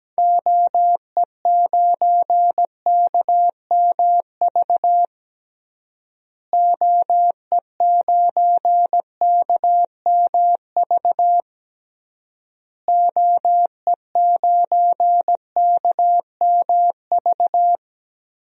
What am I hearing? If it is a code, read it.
Morse code, heard as a keyed tone, about 17 wpm, 700 Hz. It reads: OE9KMV OE9KMV OE9KMV
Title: DXCC-100 Call Signs - 6 Characters - Repeated 3x - 17wpm